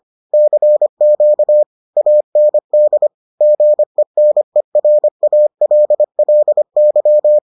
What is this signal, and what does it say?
Morse code, heard as a keyed tone, about 25 wpm, 600 Hz. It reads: CQ AND GENERALLY